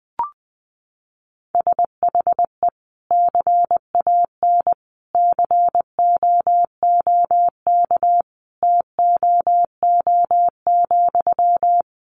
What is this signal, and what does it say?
Morse code, heard as a keyed tone, about 20 wpm, 700 Hz. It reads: SHE CAN COOK TOO,
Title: Sentences from Top 700 Words 20wpm